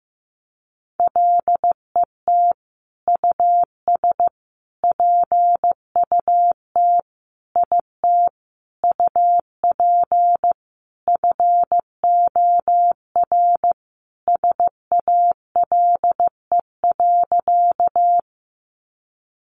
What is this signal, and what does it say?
Morse code, heard as a keyed tone, about 15 wpm, 700 Hz. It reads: LET US PUT IT UP FOR SALE.
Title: Sentences from Top 1400 Words 15wpm